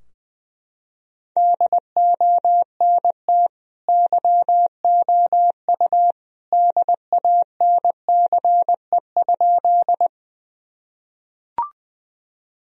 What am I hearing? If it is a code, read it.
Morse code, heard as a keyed tone, about 21 wpm, 700 Hz. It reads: DONT YOU DANCE?
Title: Sentences from Top 1000 Words - Review Part 4 - 20wpm